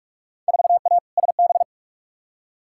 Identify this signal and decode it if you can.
Morse code, heard as a keyed tone, about 46 wpm, 700 Hz. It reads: 4A SB
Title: Field Day Exchanges Encore - 45wpm